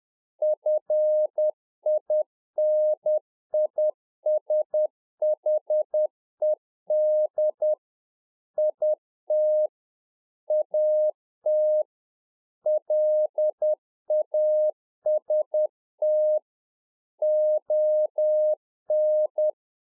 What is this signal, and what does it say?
Morse code, heard as a keyed tone, about 10 wpm, 610 Hz. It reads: FINISHED IT AT LAST ON
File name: 2011-04-mp3 / A SMALL TRANCEIVER 1 10WPM